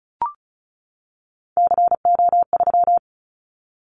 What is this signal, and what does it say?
Morse code, heard as a keyed tone, about 36 wpm, 700 Hz. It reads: CO3
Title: Callsign Prefixes for Top 100 DXCC - 35wpm